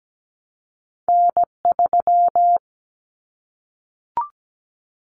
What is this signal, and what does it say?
Morse code, heard as a keyed tone, about 17 wpm, 700 Hz. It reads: N3
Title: Callsign Prefixes for Top 25 DXCC - Repeated 3x - 17wpm